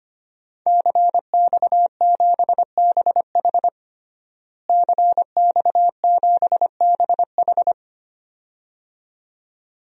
Morse code, heard as a keyed tone, about 25 wpm, 700 Hz. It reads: CX7BH CX7BH
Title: DXCC-100 Call Signs - 5 Characters - Repeated 2x - 25wpm